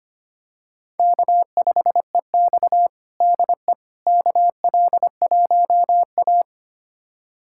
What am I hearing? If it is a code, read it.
Morse code, heard as a keyed tone, about 25 wpm, 700 Hz. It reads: K5EX DE KL1A
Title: USA Callsign DE Callsign - 4 Characters - 25wpm